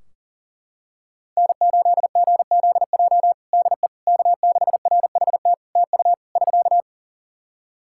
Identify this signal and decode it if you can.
Morse code, heard as a keyed tone, about 37 wpm, 700 Hz. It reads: N8GZJ DE K6RHT TU <SK>
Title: Practice QSO - Exchange 4 - Ending - 40wpm